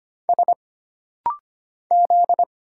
Morse code, heard as a keyed tone, about 25 wpm, 700 Hz. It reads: S Z
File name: Single Character (Letter:Number:Punctuation:Prosign) - Rapid-Fire - 25wpm